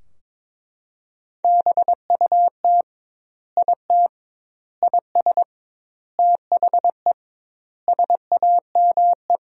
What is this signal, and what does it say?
Morse code, heard as a keyed tone, about 22 wpm, 700 Hz. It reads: BUT IT IS THE SAME
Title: 2x Word-Spacing - Sentences from Top 300 Words - 22wpm